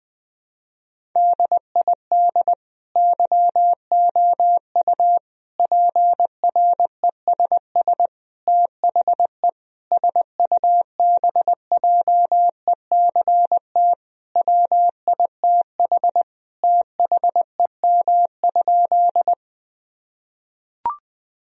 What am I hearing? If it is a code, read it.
Morse code, heard as a keyed tone, about 20 wpm, 700 Hz. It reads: DID YOU PRESS THE SUBJECT WITH THEM?
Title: Sentences from Top 1000 Words - Review Part 1 - 20wpm